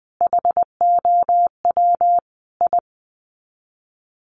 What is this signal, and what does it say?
Morse code, heard as a keyed tone, about 20 wpm, 700 Hz. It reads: HOW I
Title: Sets of 5 Words Encore2 20wpm